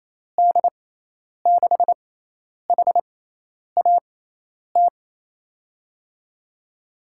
Morse code, heard as a keyed tone, about 28 wpm, spaced as for 10 wpm, 700 Hz. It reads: D6HAT
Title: USA Call Signs - 6 Characters - 6x Character Spacing - 28wpm